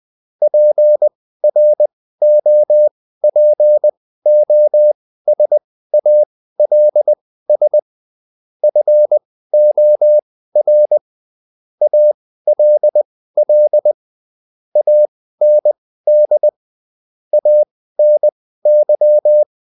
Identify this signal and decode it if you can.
Morse code, heard as a keyed tone, about 20 wpm, spaced as for 14 wpm, 600 Hz. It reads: PROPOSALS FOR ALL AND ANY